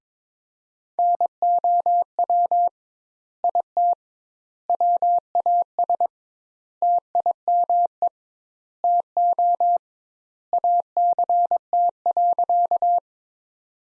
Morse code, heard as a keyed tone, about 21 wpm, 700 Hz. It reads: NOW IT WAS TIME TO ACT.
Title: Sentences Top 200 Words - 22wpm